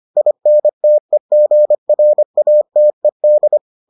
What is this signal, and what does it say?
Morse code, heard as a keyed tone, about 25 wpm, 600 Hz. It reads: INTEGRATED